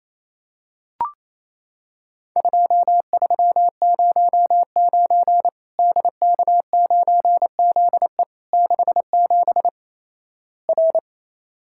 Morse code, heard as a keyed tone, about 28 wpm, 700 Hz. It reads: 2309 DK9ZE 67 R
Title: Worked All Europe DX Contest - QTC Exchange - 28wpm